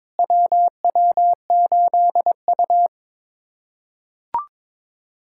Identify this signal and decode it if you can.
Morse code, heard as a keyed tone, about 22 wpm, 700 Hz. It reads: WW8U
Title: USA Callsign DE Callsign - 4 Characters - 22wpm